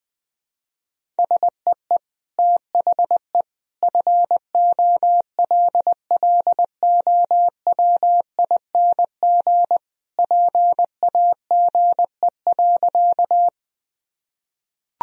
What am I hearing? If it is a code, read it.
Morse code, heard as a keyed tone, about 21 wpm, 700 Hz. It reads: SEE THE FOLLOWING PAGE.
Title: Sentences from Top 1000 Words - Review Part 1 - 20wpm